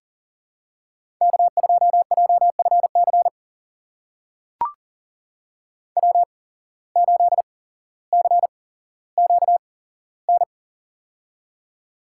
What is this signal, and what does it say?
Morse code, heard as a keyed tone, about 39 wpm, spaced as for 17 wpm, 700 Hz. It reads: K2JFC W8CQN
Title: USA Call Signs - 5 Characters - 8x Character Spacing - 40wpm